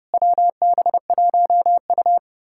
Morse code, heard as a keyed tone, about 33 wpm, 700 Hz. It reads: WB1U